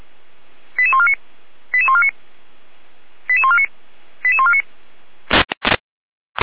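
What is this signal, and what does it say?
Five-tone selective call (ZVEI), heard as quick runs of five tones, tones of about 70 ms. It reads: 88149 88149 88149 88149
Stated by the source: Five-tone ZVEI selective call received by SDR.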